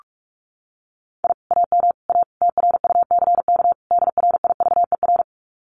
Morse code, heard as a keyed tone, about 44 wpm, 700 Hz. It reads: I AM A TRUCK DRIVER
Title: Common QSO Phrases Encore 45wpm